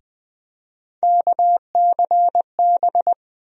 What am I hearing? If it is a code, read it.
Morse code, heard as a keyed tone, about 21 wpm, 700 Hz. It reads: KCB
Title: USA Call Sign Postfixes - 3 Letters 20wpm